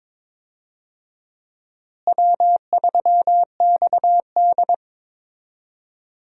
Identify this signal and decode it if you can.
Morse code, heard as a keyed tone, about 22 wpm, 700 Hz. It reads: W3XD
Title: USA Call Signs - 4 Characters - Repeated 3x - 22wpm